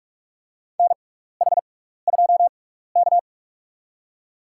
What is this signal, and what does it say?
Morse code, heard as a keyed tone, about 47 wpm, spaced as for 16 wpm, 700 Hz. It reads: NH2K
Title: USA Call Signs - 4 Characters - 6x Character Spacing - 45wpm